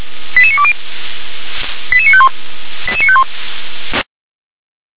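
Five-tone selective call (ZVEI), heard as quick runs of five tones, tones of about 70 ms. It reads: 88920 88951 88951
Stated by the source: ZVEI five-tone selective call received on SDR.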